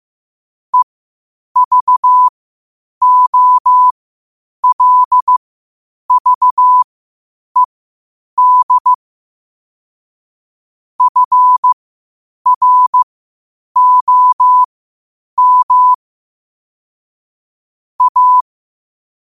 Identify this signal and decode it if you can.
Morse code, heard as a keyed tone, about 15 wpm, spaced as for 8 wpm, 1000 Hz. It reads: EVOLVED FROM A
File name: qotdfiles / 8wpm